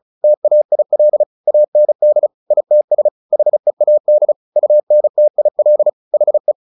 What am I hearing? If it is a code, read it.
Morse code, heard as a keyed tone, about 34 wpm, 600 Hz. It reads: TAIL AND ITS HEAD UNTIL HE
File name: the old man and the sea 2 35wpm